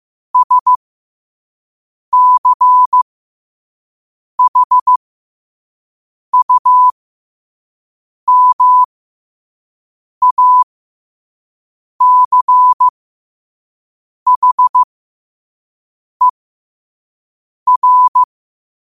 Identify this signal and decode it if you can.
Morse code, heard as a keyed tone, about 15 wpm, spaced as for 5 wpm, 1000 Hz. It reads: SCHUMACHER
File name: qotdfiles / 5wpm